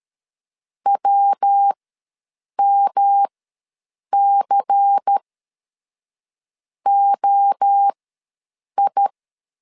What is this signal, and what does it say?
Morse code, heard as a keyed tone, about 13 wpm, spaced as for 6 wpm, 800 Hz. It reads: WMCOI
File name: LEZIONI / 5 LBVC / 5 LBVC ws 12wpm 3159